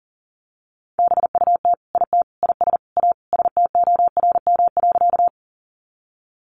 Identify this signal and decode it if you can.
Morse code, heard as a keyed tone, about 37 wpm, 700 Hz. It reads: BUT IT IS A STORM.
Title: Sentences from Top 1500 Words 40wpm